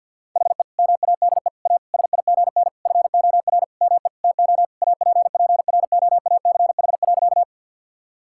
Morse code, heard as a keyed tone, about 49 wpm, 700 Hz. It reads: HE MADE A SIGN FOR ME TO APPROACH.
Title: Sentences from Top 1000 Words - Review Part 6 - 50wpm